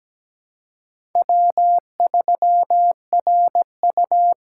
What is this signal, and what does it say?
Morse code, heard as a keyed tone, about 17 wpm, 700 Hz. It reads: W3RU
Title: USA Call Signs - 4 Characters - 2x Character Spacing - 17wpm